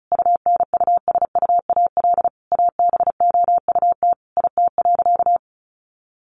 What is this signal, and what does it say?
Morse code, heard as a keyed tone, about 34 wpm, 700 Hz. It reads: UNUSUAL ABOUT IT.
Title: Sentences from Top 1400 Words 35wpm